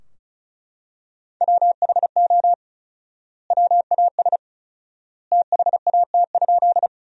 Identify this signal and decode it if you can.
Morse code, heard as a keyed tone, about 36 wpm, 700 Hz. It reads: WHO WAS THAT?
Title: 4x Word-Spacing - Sentences from Top 100 Words - 35wpm